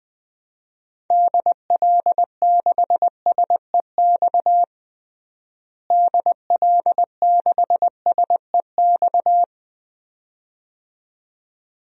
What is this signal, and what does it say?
Morse code, heard as a keyed tone, about 20 wpm, 700 Hz. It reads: DL6SEX DL6SEX
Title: DXCC-100 Call Signs - 6 Characters - Repeated 2x - 20wpm